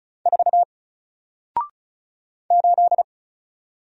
Morse code, heard as a keyed tone, about 33 wpm, 700 Hz. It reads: 4 8
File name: Single-Letter-Number-35wpm